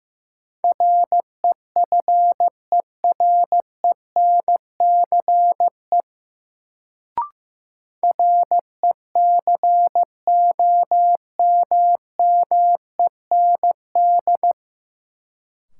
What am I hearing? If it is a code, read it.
Morse code, heard as a keyed tone, about 15 wpm, 700 Hz. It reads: REFERENCE RECOMMEND
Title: Sentences from Top 1200 Words - New Word Drill Alt - 15wpm